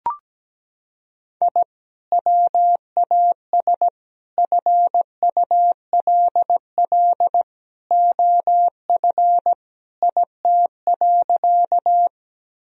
Morse code, heard as a keyed tone, about 17 wpm, 700 Hz. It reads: I WAS FULL OF IT.
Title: Sentences from Top 1000 Words - Review Part 3 - 17wpm